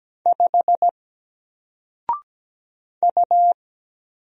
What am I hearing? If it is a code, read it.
Morse code, heard as a keyed tone, about 17 wpm, 700 Hz. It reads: 5 U
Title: Single Letter-Number 17wpm